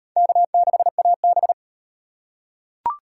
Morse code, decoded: K6AB

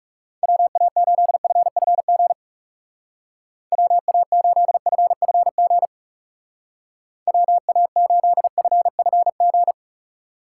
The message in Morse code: WA8FFG WA8FFG WA8FFG

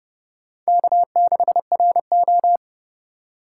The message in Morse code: K6RO

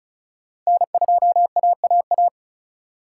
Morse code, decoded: N2AAA

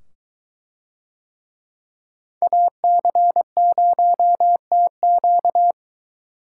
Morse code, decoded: AC0TQ

